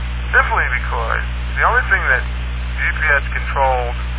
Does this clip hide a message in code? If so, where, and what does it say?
Yes: Emergency Alert System (SAME) tones from 0.2 to 3.5 s, heard as harsh buzzing bursts, with speech.